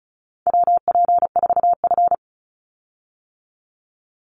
Morse code, heard as a keyed tone, about 36 wpm, 700 Hz. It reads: WP4F
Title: Call Signs - 35wpm